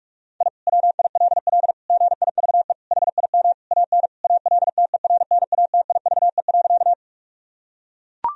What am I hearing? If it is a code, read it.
Morse code, heard as a keyed tone, about 41 wpm, 700 Hz. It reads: I WILL GIVE HIM AN ALTERNATIVE.
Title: Sentences from Top 1000 Words - Review Part 4 - 45wpm